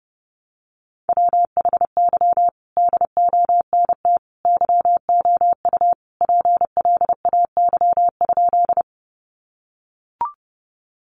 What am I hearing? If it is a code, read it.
Morse code, heard as a keyed tone, about 30 wpm, 700 Hz. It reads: WHY DONT YOU PLAY?
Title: Sentences from Top 300 Words 30wpm